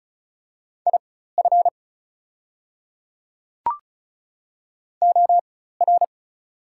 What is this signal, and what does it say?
Morse code, heard as a keyed tone, about 33 wpm, spaced as for 16 wpm, 690 Hz. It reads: IF OR
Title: 4x Character Spacing - Top 100 Words - 35wpm